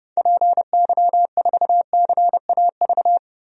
Morse code, heard as a keyed tone, about 30 wpm, 700 Hz. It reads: PY4CAV
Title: DXCC-100 Call Signs - 6 Characters - 30wpm